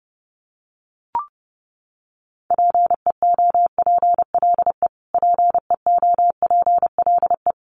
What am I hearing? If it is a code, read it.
Morse code, heard as a keyed tone, about 30 wpm, 700 Hz. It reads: PEOPLE PEOPLE
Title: Top 200 Words Repeated 30wpm